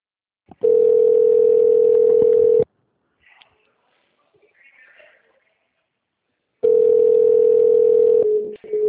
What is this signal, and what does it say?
Ringback tone (North America), 440 + 480 Hz.